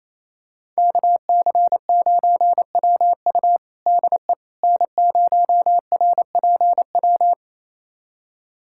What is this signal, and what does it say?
Morse code, heard as a keyed tone, about 28 wpm, 700 Hz. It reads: KC9WU DE N0RPW
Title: USA Callsign DE Callsign - 5 Characters - 28wpm